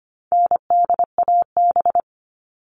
Morse code, heard as a keyed tone, about 25 wpm, 700 Hz. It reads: NDAB
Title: All Cut Numbers - 4-Digit 25wpm